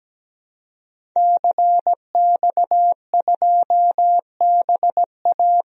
Morse code, heard as a keyed tone, about 17 wpm, 700 Hz. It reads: CX2BA
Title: DXCC-100 Call Signs - 5 Characters - Repeated 2x - 17wpm